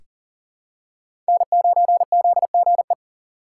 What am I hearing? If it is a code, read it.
Morse code, heard as a keyed tone, about 38 wpm, 700 Hz. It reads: N9ZGE
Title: USA Call Signs - 6x Character Spacing - 40wpm